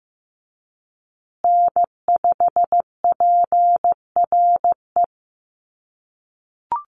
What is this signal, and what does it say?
Morse code, heard as a keyed tone, about 15 wpm, 700 Hz. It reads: N5PRE
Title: USA Call Signs Repeated 2x 15wpm